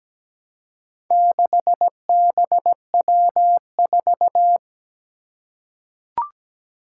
Morse code, heard as a keyed tone, about 17 wpm, 700 Hz. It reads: 6BW4
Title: Four Challenging Letter-Number Combinations 17wpm